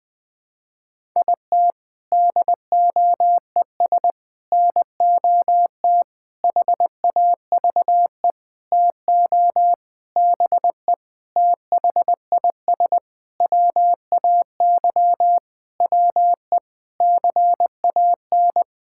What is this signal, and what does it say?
Morse code, heard as a keyed tone, about 20 wpm, 700 Hz. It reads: IT DOES NOT HAVE TO BE THIS WAY WE CAN
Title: Sentences with 7-Word Phrases - 20wpm